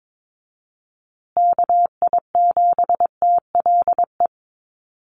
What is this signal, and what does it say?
Morse code, heard as a keyed tone, about 22 wpm, 700 Hz. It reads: KI7TLE
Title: USA Call Signs Repeated 2x 22wpm